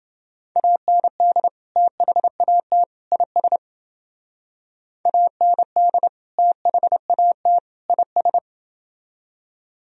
Morse code, heard as a keyed tone, about 29 wpm, 700 Hz. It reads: AND THAT IS AND THAT IS